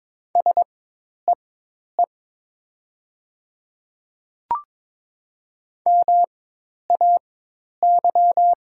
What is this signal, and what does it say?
Morse code, heard as a keyed tone, about 22 wpm, spaced as for 10 wpm, 700 Hz. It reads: SEE MAY
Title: 4x Character Spacing - Top 100 Words - 22wpm